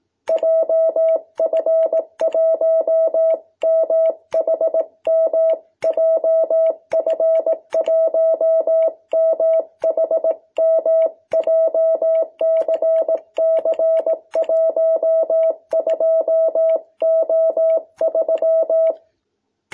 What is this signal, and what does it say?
Morse code, heard as a keyed tone, about 18 wpm, 650 Hz. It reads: JF1MHM JF1MHM JCC12O3
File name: jh8cbh / 2021cw